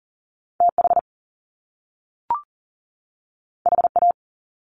Morse code, heard as a keyed tone, about 40 wpm, 700 Hz. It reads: TH HA
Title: Top Two Letter Combinations in Words 40wpm